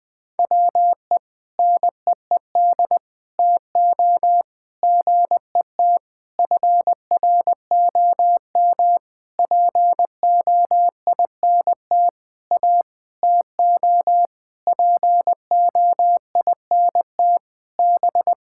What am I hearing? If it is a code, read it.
Morse code, heard as a keyed tone, about 21 wpm, 700 Hz. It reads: WE NEED TO GET FROM POINT A TO POINT B
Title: Sentences with 6-Word Phrases - 20wpm